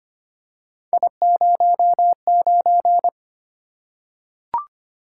Morse code, heard as a keyed tone, about 25 wpm, 700 Hz. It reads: I09
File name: Three Letter-Number Combinations - 25wpm